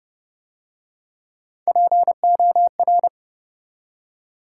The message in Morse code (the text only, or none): POR